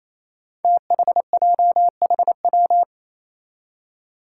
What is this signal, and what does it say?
Morse code, heard as a keyed tone, about 28 wpm, 700 Hz. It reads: THJHW
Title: Five Challenging Letter-Number Combinations 28wpm